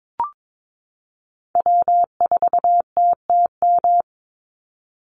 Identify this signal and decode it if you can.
Morse code, heard as a keyed tone, about 22 wpm, 700 Hz. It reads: W4TTM
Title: Call Signs Repeated 3x - 22wpm